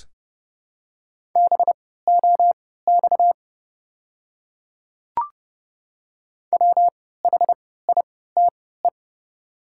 Morse code, heard as a keyed tone, about 30 wpm, spaced as for 17 wpm, 700 Hz. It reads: BOX WHITE
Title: 3x Character Spacing - Top 500 Words - 30wpm